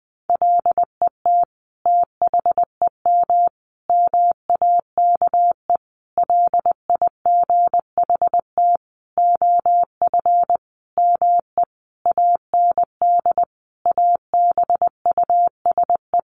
Morse code, heard as a keyed tone, about 20 wpm, 700 Hz. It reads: LET THEM MAKE LIGHT OF ME AND ABUSE
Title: Sentences from Top 1300 Words 20wpm